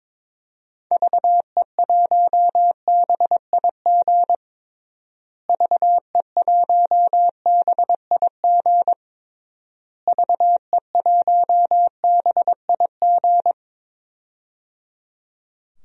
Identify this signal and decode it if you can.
Morse code, heard as a keyed tone, about 21 wpm, 700 Hz. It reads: VE1BIG VE1BIG VE1BIG
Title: DXCC-100 Call Signs - 6 Characters - Repeated 3x - 22wpm